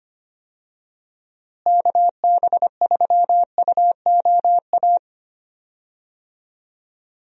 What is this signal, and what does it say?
Morse code, heard as a keyed tone, about 25 wpm, 700 Hz. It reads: KB3UOA
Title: USA Call Signs - 6 Characters - Repeated 2x - 25wpm